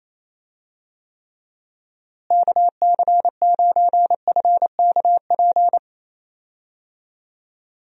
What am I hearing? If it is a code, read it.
Morse code, heard as a keyed tone, about 27 wpm, 700 Hz. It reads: KC9FKP
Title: USA Call Signs - 6 Characters - Repeated 2x - 28wpm